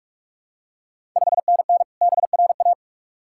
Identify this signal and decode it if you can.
Morse code, heard as a keyed tone, about 42 wpm, 690 Hz. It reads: 5NN BRA
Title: Ontario QSO Party Exchange 45wpm